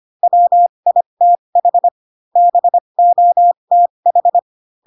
Morse code, heard as a keyed tone, about 25 wpm, spaced as for 22 wpm, 700 Hz. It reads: WITH BOTH